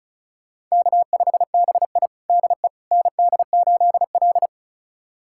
Morse code, heard as a keyed tone, about 36 wpm, 700 Hz. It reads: K5BI DE ND8L
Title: USA Callsign DE Callsign - 4 Characters - 35wpm